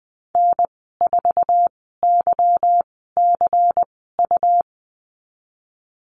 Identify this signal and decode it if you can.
Morse code, heard as a keyed tone, about 19 wpm, spaced as for 14 wpm, 700 Hz. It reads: N4YCU
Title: 154 Lesson 25g - Letter p - Call Signs 20wpm